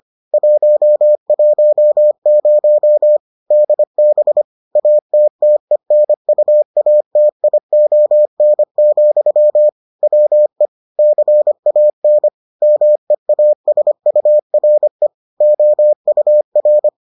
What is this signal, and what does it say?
Morse code, heard as a keyed tone, about 25 wpm, 600 Hz. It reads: 110 DB ATTENUATION, WE CAN MEASURE OUR